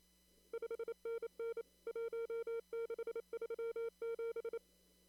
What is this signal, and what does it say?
Morse code, heard as a keyed tone, about 28 wpm, 460 Hz. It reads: 5NN 1637
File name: CW / kt5j wpx cw-op-k5pi